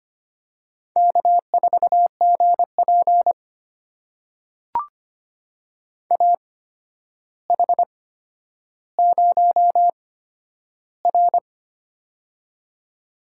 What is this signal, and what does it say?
Morse code, heard as a keyed tone, about 25 wpm, spaced as for 11 wpm, 700 Hz. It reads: K4GP AH0R
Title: USA Call Signs - 8x Character Spacing - 25wpm